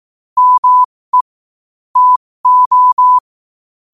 Morse code, heard as a keyed tone, about 18 wpm, spaced as for 15 wpm, 1000 Hz. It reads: ME TO
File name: qotdfiles / 15wpm